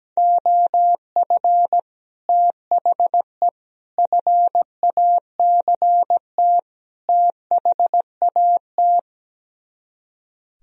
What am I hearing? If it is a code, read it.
Morse code, heard as a keyed tone, about 17 wpm, 700 Hz. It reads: OF THE FACT THAT
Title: Speed-Racing - Sets of 4 Words 17wpm